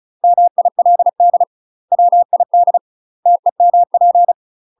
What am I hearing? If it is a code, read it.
Morse code, heard as a keyed tone, about 34 wpm, 700 Hz. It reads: MILD WID TEMP
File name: UK MASTER QSOs TRAINING / 35WPM / UK MASTER QSOs 0004